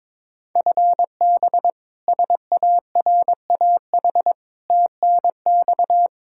Morse code, heard as a keyed tone, about 22 wpm, 700 Hz. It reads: FB SARAH TNX
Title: K1USN SST QSO Practice 22wpm